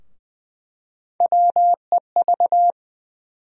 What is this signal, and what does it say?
Morse code, heard as a keyed tone, about 20 wpm, 700 Hz. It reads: WEV
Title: USA Call Sign Postfixes - 3 Letters 20wpm